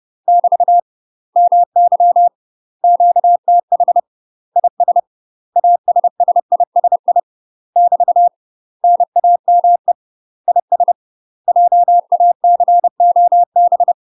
Morse code, heard as a keyed tone, about 30 wpm, 700 Hz. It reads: <BT> MY QTH IS ASSISI <BT> NAME IS JACOB
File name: UK MASTER QSOs TRAINING / 30WPM / UK MASTER QSOs 0004